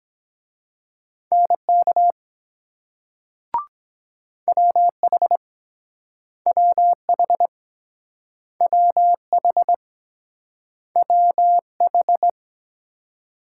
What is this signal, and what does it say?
Morse code, heard as a keyed tone, about 23 wpm, 700 Hz. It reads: NK WH WH WH WH